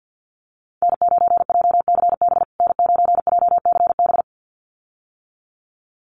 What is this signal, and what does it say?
Morse code, heard as a keyed tone, about 51 wpm, 700 Hz. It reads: N9JCB N9JCB
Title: USA Call Signs Repeated 2x 50wpm